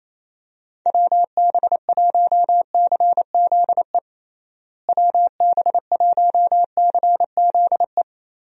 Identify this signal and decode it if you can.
Morse code, heard as a keyed tone, about 27 wpm, 700 Hz. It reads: WB1CZE WB1CZE
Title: USA Call Signs - 6 Characters - Repeated 3x - 28wpm